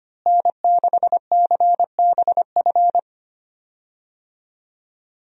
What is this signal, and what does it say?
Morse code, heard as a keyed tone, about 25 wpm, 690 Hz. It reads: N6CBF
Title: USA Call Signs - 5 Characters - Repeated 2x - 25wpm